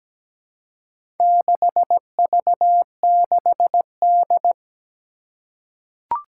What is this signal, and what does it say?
Morse code, heard as a keyed tone, about 17 wpm, 700 Hz. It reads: 6V6D
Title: Four Challenging Letter-Number Combinations 17wpm